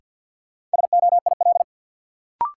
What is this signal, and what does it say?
Morse code, heard as a keyed tone, about 53 wpm, 700 Hz. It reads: SOIL
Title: Sentences from Top 1000 Words - Review Part 4 - New Word Drill - 50wpm